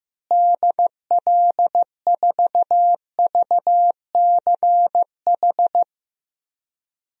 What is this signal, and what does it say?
Morse code, heard as a keyed tone, about 15 wpm, 700 Hz. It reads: DL4VCH
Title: DXCC-100 Call Signs - 6 Characters - Repeated 2x - 15wpm